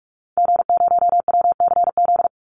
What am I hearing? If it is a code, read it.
Morse code, heard as a keyed tone, about 44 wpm, 700 Hz. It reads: G0WCZ
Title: Call Signs Repeated 3x - 45wpm